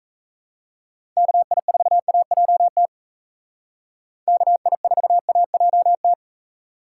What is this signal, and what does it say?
Morse code, heard as a keyed tone, about 34 wpm, 700 Hz. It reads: KI4AJT KI4AJT
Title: Speed Racing - USA Call Signs 28wpm